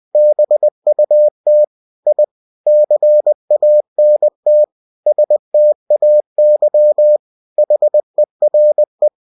Morse code, heard as a keyed tone, about 20 wpm, 600 Hz. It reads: BUT I CANT STAY HERE